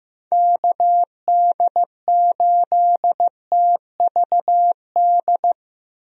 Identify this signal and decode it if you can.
Morse code, heard as a keyed tone, about 15 wpm, 700 Hz. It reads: KD8TVD
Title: USA Call Signs - 6 Characters - 15wpm